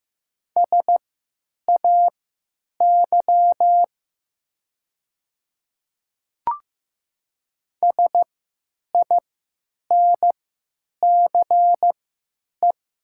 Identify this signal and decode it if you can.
Morse code, heard as a keyed tone, about 15 wpm, spaced as for 9 wpm, 690 Hz. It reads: SAY SINCE
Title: 3x Character Spacing - Top 300 Words - 15wpm